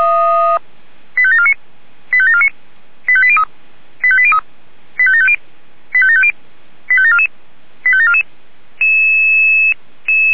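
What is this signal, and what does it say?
Five-tone selective call (ZVEI), heard as quick runs of five tones, tones of about 70 ms. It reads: 85749 85749 85803 85803 85750 85750 85744 85744
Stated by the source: ZVEI five-tone selective call 85725, received on an SDR.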